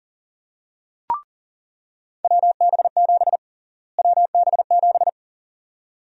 Morse code, heard as a keyed tone, about 40 wpm, 700 Hz. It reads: WB7 WB7
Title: USA Call Sign Prefixes - Repeated 2x - 40wpm